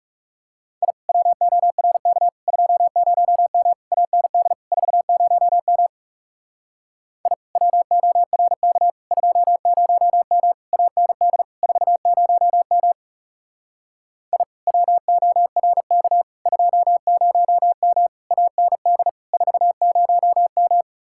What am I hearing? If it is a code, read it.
Morse code, heard as a keyed tone, about 40 wpm, 700 Hz. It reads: I WORK 20M AND 40M I WORK 20M AND 40M I WORK 20M AND 40M